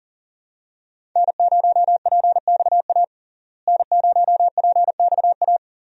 Morse code, heard as a keyed tone, about 40 wpm, 700 Hz. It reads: N0PXA N0PXA
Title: USA Call Signs - 5 Characters - Repeated 3x - 40wpm